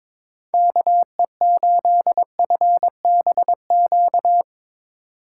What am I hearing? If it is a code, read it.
Morse code, heard as a keyed tone, about 22 wpm, 700 Hz. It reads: KE8FBQ
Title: USA Call Signs - 6 Characters - 6x Character Spacing - 22wpm